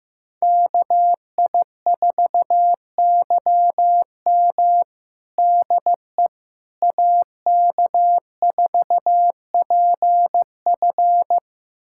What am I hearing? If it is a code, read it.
Morse code, heard as a keyed tone, about 15 wpm, 700 Hz. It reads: KI4YM DE AK4PF
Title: USA Callsign DE Callsign - 5 Characters - 15wpm